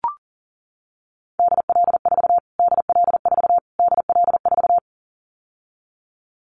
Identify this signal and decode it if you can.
Morse code, heard as a keyed tone, about 40 wpm, 700 Hz. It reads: DL4 DL4 DL4
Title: Callsign Prefixes for Top 25 DXCC - Repeated 3x - 40wpm